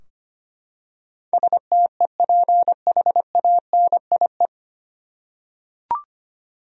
Morse code, heard as a keyed tone, about 25 wpm, 700 Hz. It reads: STEPHANIE
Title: USA Common Names - 25wpm